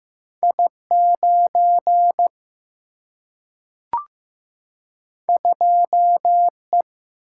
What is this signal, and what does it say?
Morse code, heard as a keyed tone, about 15 wpm, 700 Hz. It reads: I9 2E
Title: Single-Letter Single-Number 15wpm